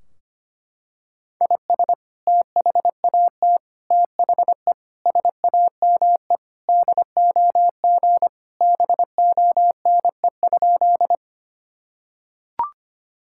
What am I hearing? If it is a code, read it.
Morse code, heard as a keyed tone, about 25 wpm, 700 Hz. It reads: IS THAT THE SAME DOG BONE?
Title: Sentences from Top 1600 Words 25wpm